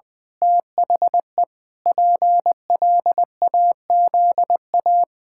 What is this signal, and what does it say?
Morse code, heard as a keyed tone, about 20 wpm, 700 Hz. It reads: THE PLAZA
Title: Pangram Challenge v2 - 20wpm